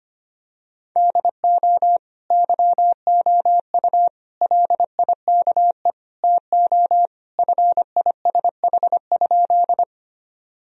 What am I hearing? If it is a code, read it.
Morse code, heard as a keyed tone, about 25 wpm, 700 Hz. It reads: DO YOU LIKE TO FISH?
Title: Sentences from Top 1000 Words - Review Part 1 - 25wpm